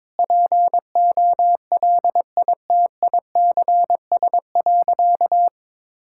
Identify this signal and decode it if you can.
Morse code, heard as a keyed tone, about 22 wpm, 700 Hz. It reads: POLITICS.
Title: Sentences from Top 1000 Words - Review Part 6 - 22wpm